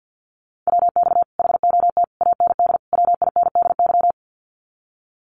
Morse code, heard as a keyed tone, about 50 wpm, 700 Hz. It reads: WX HOT AND WINDY
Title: Common QSO Phrases Encore 50wpm